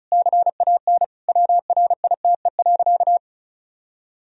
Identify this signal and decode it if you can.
Morse code, heard as a keyed tone, about 36 wpm, 700 Hz. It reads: CAN WRITE.
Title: Sentences from Top 1000 Words - Review Part 1 - 35wpm